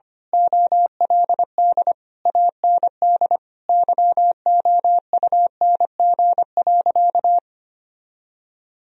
Morse code, heard as a keyed tone, about 25 wpm, 700 Hz. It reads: OLD AND YOUNG.
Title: Sentences from Top 1000 Words - Review Part 3 - 25wpm